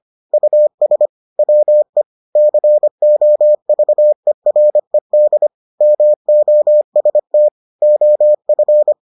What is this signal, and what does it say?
Morse code, heard as a keyed tone, about 25 wpm, 600 Hz. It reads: US WE COVERED MOST OF